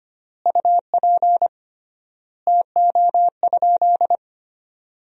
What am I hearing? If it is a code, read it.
Morse code, heard as a keyed tone, about 25 wpm, 700 Hz. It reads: UP TO?
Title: 3x Word-Spacing - Sentences from Top 100 Words - 25wpm